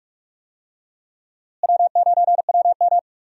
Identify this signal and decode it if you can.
Morse code, heard as a keyed tone, about 46 wpm, 700 Hz. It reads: W9WM
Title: USA Call Signs - 4 Characters - Repeated 3x - 45wpm